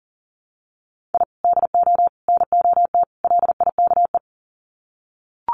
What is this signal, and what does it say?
Morse code, heard as a keyed tone, about 40 wpm, 700 Hz. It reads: I DO NOT LIKE
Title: Sets of 4 Words 40wpm